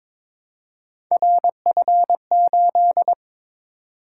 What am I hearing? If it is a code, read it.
Morse code, heard as a keyed tone, about 22 wpm, 700 Hz. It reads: RF8
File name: practice / 22wpm / Three Letter-Number Combinations - 22wpm